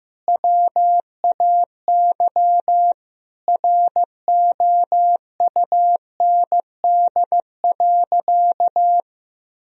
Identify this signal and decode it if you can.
Morse code, heard as a keyed tone, about 15 wpm, 700 Hz. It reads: WAY ROUND.
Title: Sentences from Top 1000 Words - Review Part 3 - 15wpm